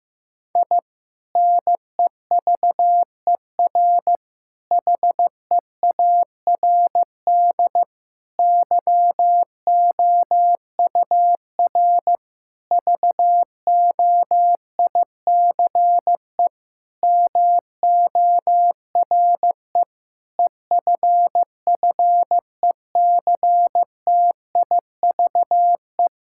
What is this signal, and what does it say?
Morse code, heard as a keyed tone, about 15 wpm, 700 Hz. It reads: I NEVER HEARD YOUR VOICE MORE EFFECTIVE
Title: Sentences from Top 1000 Words - Review Part 6 - 15wpm